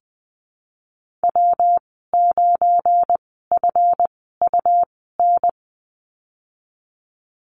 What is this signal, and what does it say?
Morse code, heard as a keyed tone, about 20 wpm, spaced as for 15 wpm, 700 Hz. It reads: W9FUN
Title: 179 Lesson 29g - Slash - Call Signs 20wpm